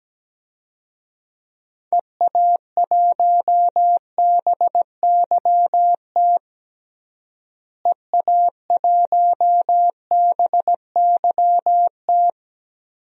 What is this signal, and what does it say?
Morse code, heard as a keyed tone, about 17 wpm, 700 Hz. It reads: EA1BYT EA1BYT
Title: DXCC-100 Call Signs - 6 Characters - Repeated 3x - 17wpm